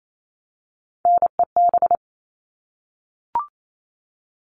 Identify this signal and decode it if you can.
Morse code, heard as a keyed tone, about 28 wpm, 700 Hz. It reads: NEB
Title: All Cut Numbers - 3-Digit 28wpm